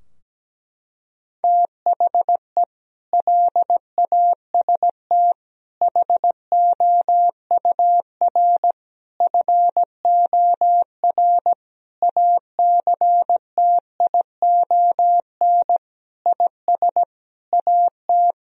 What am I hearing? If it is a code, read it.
Morse code, heard as a keyed tone, about 17 wpm, 700 Hz. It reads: THE LAST HOUR FOR ACTION IS AT
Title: Sentences from Top 1000 Words - Review Part 4 - 17wpm